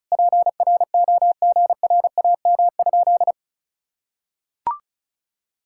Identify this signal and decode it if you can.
Morse code, heard as a keyed tone, about 36 wpm, 700 Hz. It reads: PROGRAM?
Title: Sentences from Top 1000 Words - Review Part 5 - 35wpm